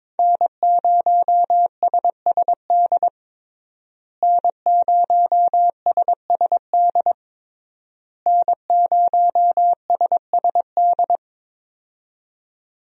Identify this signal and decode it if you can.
Morse code, heard as a keyed tone, about 22 wpm, 700 Hz. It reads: N0SSD N0SSD N0SSD
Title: USA Call Signs - 5 Characters - Repeated 3x - 22wpm